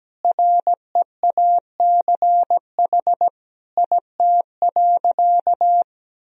Morse code, heard as a keyed tone, about 17 wpm, 700 Hz. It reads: REACH IT.
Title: Sentences from Top 400 Words and Abbreviations 17wpm